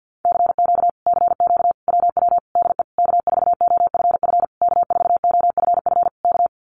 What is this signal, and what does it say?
Morse code, heard as a keyed tone, about 50 wpm, 700 Hz. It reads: CQ CQ WW DE K4OFF K4OFF K
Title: Common QSO Phrases Encore 50wpm